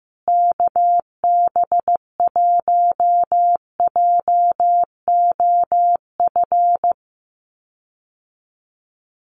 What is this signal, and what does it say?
Morse code, heard as a keyed tone, about 15 wpm, 700 Hz. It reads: KB1JOF
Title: USA Call Signs Repeated 2x 15wpm